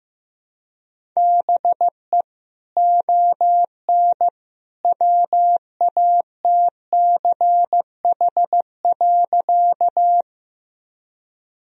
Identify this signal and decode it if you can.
Morse code, heard as a keyed tone, about 15 wpm, 700 Hz. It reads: BE ON WATCH.
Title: Sentences from Top 1000 Words - Review Part 1 - 15wpm